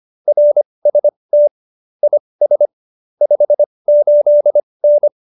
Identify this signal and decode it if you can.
Morse code, heard as a keyed tone, about 25 wpm, spaced as for 20 wpm, 590 Hz. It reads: RST IS 58N
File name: qso215b 20